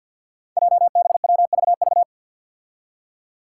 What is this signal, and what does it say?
Morse code, heard as a keyed tone, about 49 wpm, 700 Hz. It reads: JBWVV
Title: Five Challenging Letter-Number Combinations 50wpm